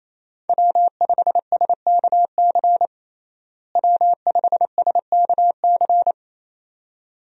Morse code, heard as a keyed tone, about 28 wpm, 700 Hz. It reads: W5SKC W5SKC
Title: USA Call Signs - 5 Characters - Repeated 2x - 28wpm